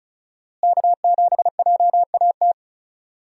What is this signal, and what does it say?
Morse code, heard as a keyed tone, about 34 wpm, 700 Hz. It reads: K7JAT
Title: USA Call Signs - 5 Characters - Repeated 2x - 35wpm